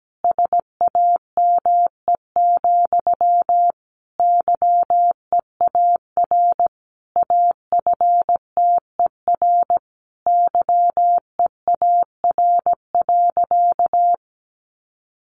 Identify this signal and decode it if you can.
Morse code, heard as a keyed tone, about 17 wpm, 700 Hz. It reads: SAME, YEAR AFTER YEAR.
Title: Sentences from Top 300 Words 17wpm